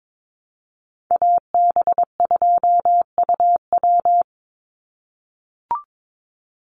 Morse code, heard as a keyed tone, about 22 wpm, 700 Hz. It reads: AB2UW
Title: USA Call Signs 22wpm